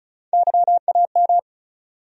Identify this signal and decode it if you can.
Morse code, heard as a keyed tone, about 36 wpm, 700 Hz. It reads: YAM
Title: USA Call Sign Postfixes - 3 Letters 35wpm